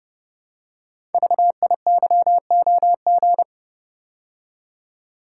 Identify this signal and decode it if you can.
Morse code, heard as a keyed tone, about 29 wpm, 700 Hz. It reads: VIYOG
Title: Code Groups - 5 Letters - 30wpm